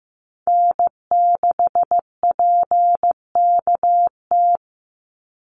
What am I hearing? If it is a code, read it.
Morse code, heard as a keyed tone, about 15 wpm, 700 Hz. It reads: N6PKT
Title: NA SOTA Chaser Callsigns 15wpm